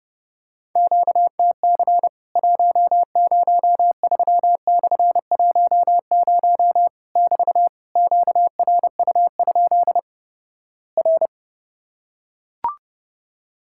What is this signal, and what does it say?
Morse code, heard as a keyed tone, about 31 wpm, 700 Hz. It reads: QTC 103/10 <BT> QRU? R
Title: Worked All Europe DX Contest - QTC Exchange - 30wpm